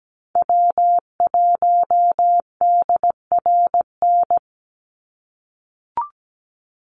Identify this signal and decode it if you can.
Morse code, heard as a keyed tone, about 17 wpm, 700 Hz. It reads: W1DRN
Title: USA Call Signs 17wpm